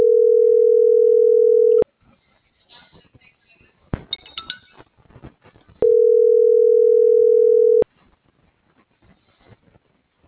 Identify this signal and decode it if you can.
Ringback tone (North America), 440 + 480 Hz.